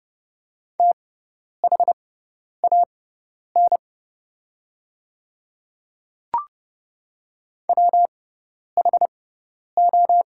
Morse code, heard as a keyed tone, about 30 wpm, spaced as for 10 wpm, 690 Hz. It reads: THAN WHO
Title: 6x Character Spacing - Top 100 Words - 30wpm